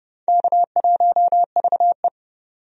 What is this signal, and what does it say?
Morse code, heard as a keyed tone, about 30 wpm, 700 Hz. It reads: K1VE
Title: USA Call Signs - 4 Characters - 30wpm